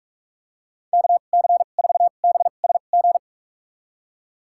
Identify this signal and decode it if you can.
Morse code, heard as a keyed tone, about 41 wpm, spaced as for 31 wpm, 700 Hz. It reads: KC4BSG
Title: USA Call Signs - 6 Characters - 2x Character Spacing - 45wpm